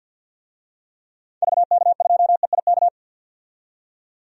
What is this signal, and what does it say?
Morse code, heard as a keyed tone, about 52 wpm, 700 Hz. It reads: VK2EIK
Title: DXCC-100 Call Signs - 6 Characters - 50wpm